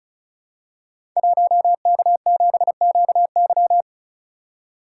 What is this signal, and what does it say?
Morse code, heard as a keyed tone, about 36 wpm, 700 Hz. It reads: 1K7QY
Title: Code Groups - 5 Letter:Numbers - 35wpm